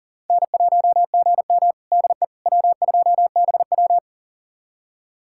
Morse code, heard as a keyed tone, about 40 wpm, 700 Hz. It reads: N1GM DE W2BW
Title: USA Callsign DE Callsign - 4 Characters - 40wpm